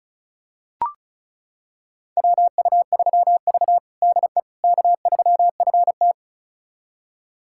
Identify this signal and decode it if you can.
Morse code, heard as a keyed tone, about 36 wpm, 700 Hz. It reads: WU3V DE K3FT
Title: USA Callsign DE Callsign - 4 Characters - 35wpm